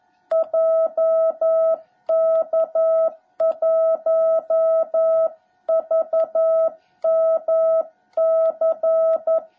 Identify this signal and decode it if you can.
Morse code, heard as a keyed tone, about 11 wpm, 660 Hz. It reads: JK1VMC
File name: jh8cbh / 2021cw / cw17b